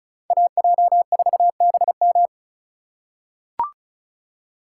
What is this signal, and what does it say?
Morse code, heard as a keyed tone, about 36 wpm, 700 Hz. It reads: AJ4BM
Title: USA Call Signs - 5 Characters - 3x Character Spacing - 35wpm